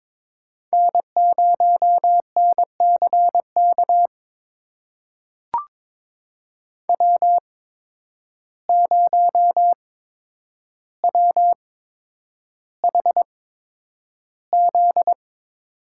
Morse code, heard as a keyed tone, about 22 wpm, spaced as for 9 wpm, 700 Hz. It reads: N0NCK W0WHZ
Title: USA Call Signs - 5 Characters - 8x Character Spacing - 22wpm